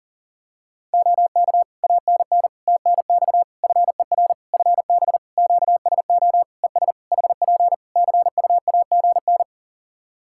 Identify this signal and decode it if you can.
Morse code, heard as a keyed tone, about 40 wpm, 700 Hz. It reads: OK ANN TNX FER FB QSO ES HP CUAGN
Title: K1USN SST QSO Practice 40wpm